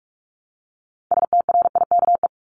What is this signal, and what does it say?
Morse code, heard as a keyed tone, about 44 wpm, 700 Hz. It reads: STRIKE
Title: Top 1501-1750 Words 45wpm